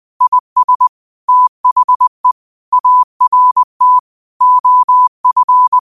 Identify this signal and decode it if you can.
Morse code, heard as a keyed tone, about 20 wpm, 1000 Hz. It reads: IS THE ART OF